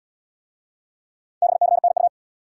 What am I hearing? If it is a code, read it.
Morse code, heard as a keyed tone, about 68 wpm, 700 Hz. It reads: BCNU